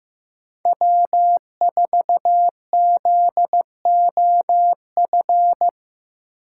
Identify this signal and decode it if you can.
Morse code, heard as a keyed tone, about 15 wpm, 700 Hz. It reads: W4ZOF
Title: Code Groups - 5 Letter:Numbers - 15wpm